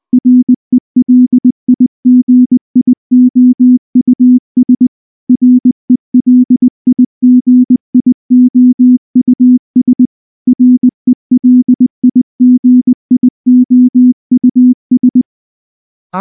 Morse code, heard as a keyed tone, about 20 wpm, 250 Hz. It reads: RELIGIOUS RELIGIOUS RELIGIOUS
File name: morse-training-20WPM